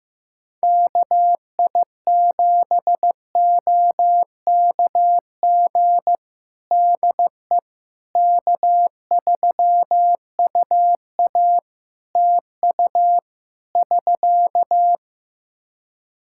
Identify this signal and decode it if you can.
Morse code, heard as a keyed tone, about 15 wpm, 700 Hz. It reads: KI7OKG DE K3UA TU <SK>
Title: Practice QSO - Exchange 4 - Ending - 15wpm